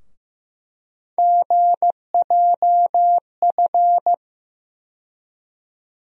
Morse code, heard as a keyed tone, about 15 wpm, 700 Hz. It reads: GJF
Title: USA Call Sign Postfixes - 3 Letters 15wpm